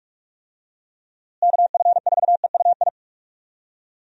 Morse code, heard as a keyed tone, about 42 wpm, 700 Hz. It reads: KF4EUI
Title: USA Call Signs - 6 Characters - Repeated 2x - 45wpm